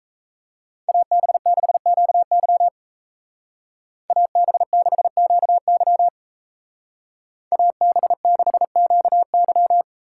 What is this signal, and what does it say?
Morse code, heard as a keyed tone, about 32 wpm, 700 Hz. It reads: AB6QY AB6QY AB6QY
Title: Speed Racing - USA Call Signs 28wpm